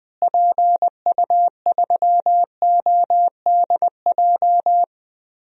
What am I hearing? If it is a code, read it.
Morse code, heard as a keyed tone, about 20 wpm, 700 Hz. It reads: PU3ODJ
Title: DXCC-100 Call Signs - 6 Characters - Repeated 3x - 20wpm